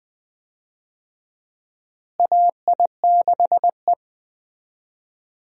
Morse code, heard as a keyed tone, about 20 wpm, 700 Hz. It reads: AI6E
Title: USA Call Signs - 4 Characters - Repeated 3x - 20wpm